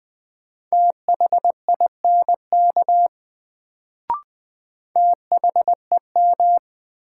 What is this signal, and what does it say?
Morse code, heard as a keyed tone, about 20 wpm, 700 Hz. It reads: THINK THEM
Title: Sets of 2 Words - New Word Drill - 20wpm